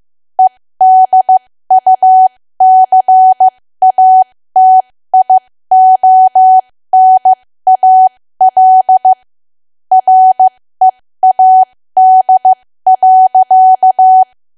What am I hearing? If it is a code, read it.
Morse code, heard as a keyed tone, about 15 wpm, spaced as for 13 wpm, 750 Hz. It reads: EDUCATIONAL READ.
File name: cwa / W1AW-Level2 / 13WPM